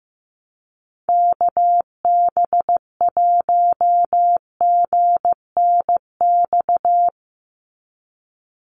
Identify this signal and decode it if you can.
Morse code, heard as a keyed tone, about 15 wpm, 700 Hz. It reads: KB1GNX
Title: USA Call Signs 15wpm